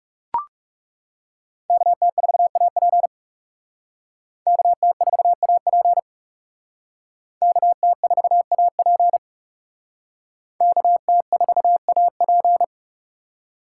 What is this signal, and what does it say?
Morse code, heard as a keyed tone, about 36 wpm, 700 Hz. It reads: KT4AP KT4AP KT4AP KT4AP
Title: Speed Racing - USA 2x2 Call Signs 30wpm